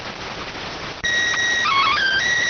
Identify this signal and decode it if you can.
SSTV (slow-scan television), PD 290.